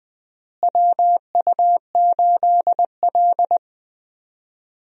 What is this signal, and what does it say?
Morse code, heard as a keyed tone, about 20 wpm, 700 Hz. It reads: WU8L
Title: USA Call Signs - 4 Characters - Repeated 3x - 20wpm